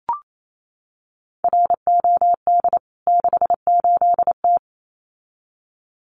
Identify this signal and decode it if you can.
Morse code, heard as a keyed tone, about 28 wpm, 700 Hz. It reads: ROD 68T
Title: CWT Exchange 28wpm